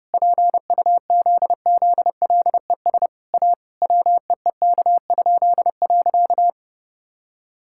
Morse code, heard as a keyed tone, about 30 wpm, 700 Hz. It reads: PUZZLES A WEEK?.
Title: Pangram Challenge v2 - 30wpm